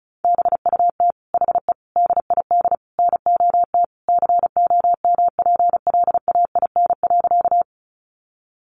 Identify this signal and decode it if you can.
Morse code, heard as a keyed tone, about 34 wpm, 700 Hz. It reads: BUT HE DID NOT COMPLAIN.
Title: Sentences from Top 1300 Words 35wpm